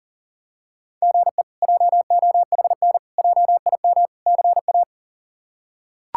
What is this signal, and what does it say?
Morse code, heard as a keyed tone, about 40 wpm, 700 Hz. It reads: GE JOHN JIM CA
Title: K1USN SST QSO Practice - 40wpm